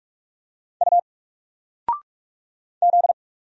Morse code, heard as a keyed tone, about 44 wpm, 700 Hz. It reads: U Z
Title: Single Character (Letter:Number:Punctuation:Prosign) 45wpm